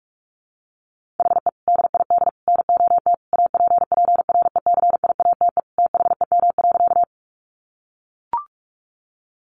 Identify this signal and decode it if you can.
Morse code, heard as a keyed tone, about 44 wpm, 700 Hz. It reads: HE DID NOT APPRECIATE THEM.
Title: Sentences from Top 1300 Words 45wpm